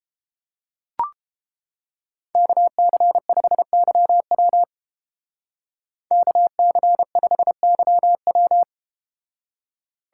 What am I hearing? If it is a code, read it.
Morse code, heard as a keyed tone, about 30 wpm, 700 Hz. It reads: KC5YW KC5YW